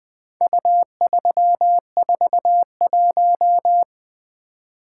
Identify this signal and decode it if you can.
Morse code, heard as a keyed tone, about 20 wpm, 700 Hz. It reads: U341
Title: Four Letter-Number Combinations - 20wpm